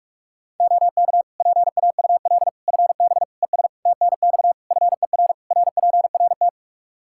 Morse code, heard as a keyed tone, about 46 wpm, 700 Hz. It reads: OK PAUL FB ES TNX FER RPRT
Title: K1USN SST QSO Practice 45wpm